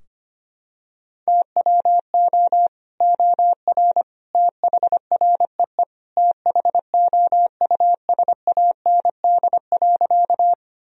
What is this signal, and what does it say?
Morse code, heard as a keyed tone, about 25 wpm, 700 Hz. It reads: TWO OR THREE THOUSAND.
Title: Sentences from Top 1000 Words - Review Part 3 - 25wpm